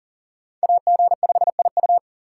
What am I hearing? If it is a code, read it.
Morse code, heard as a keyed tone, about 40 wpm, 700 Hz. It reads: AG5IU
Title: USA Call Signs - 5 Characters - 6x Character Spacing - 40wpm